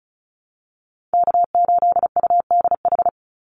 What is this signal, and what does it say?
Morse code, heard as a keyed tone, about 36 wpm, 700 Hz. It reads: K8UDH
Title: Call Signs - 35wpm